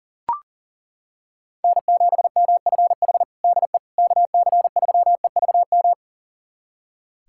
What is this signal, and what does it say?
Morse code, heard as a keyed tone, about 40 wpm, 700 Hz. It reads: N7MFH DE KC3EVM
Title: USA Callsign DE Callsign 40wpm